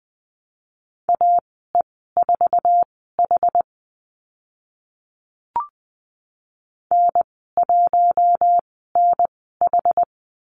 Morse code, heard as a keyed tone, about 20 wpm, spaced as for 14 wpm, 700 Hz. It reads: AE4H N1NH
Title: 076 Lesson 13g - Letter L - Call Signs 20wpm